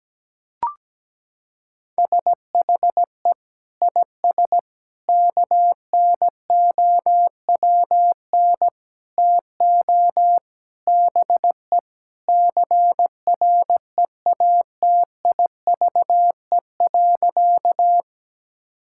Morse code, heard as a keyed tone, about 17 wpm, 700 Hz. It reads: SHE IS KNOWN TO BE CREATIVE.
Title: Sentences from Top 1000 Words - Review Part 5 - 17wpm